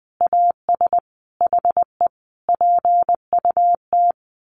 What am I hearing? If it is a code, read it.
Morse code, heard as a keyed tone, about 20 wpm, 700 Hz. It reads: AS HE PUT
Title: A Sound Of Thunder 20wpm